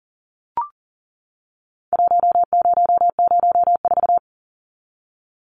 Morse code, heard as a keyed tone, about 40 wpm, 700 Hz. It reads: 1004
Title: Numbers - Contest Style - 4-Digit 40wpm